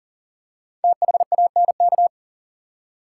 Morse code, heard as a keyed tone, about 40 wpm, 700 Hz. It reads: THANK
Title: Five Letter Words - 40wpm